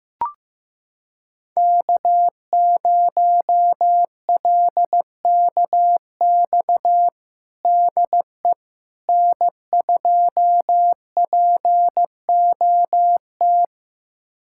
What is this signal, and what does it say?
Morse code, heard as a keyed tone, about 15 wpm, 700 Hz. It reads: K0LKX DE N2POT
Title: USA Callsign DE Callsign - 5 Characters - 15wpm